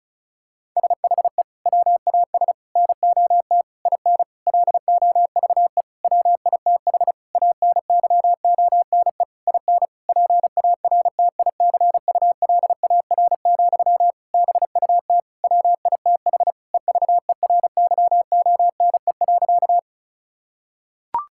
Morse code, heard as a keyed tone, about 36 wpm, 700 Hz. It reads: SHE WAS NOT IN LOVE WITH ANYONE IN PARTICULAR, BUT WITH EVERYONE.
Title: Sentences from Top 1000 Words - Review Part 5 - 35wpm